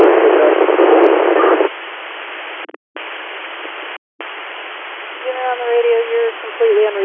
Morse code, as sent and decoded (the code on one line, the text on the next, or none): none